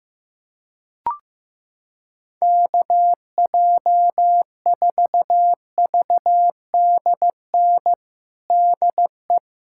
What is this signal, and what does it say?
Morse code, heard as a keyed tone, about 16 wpm, 700 Hz. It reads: KJ4VDN DE
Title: USA Callsign DE Callsign - 6 Characters - 15wpm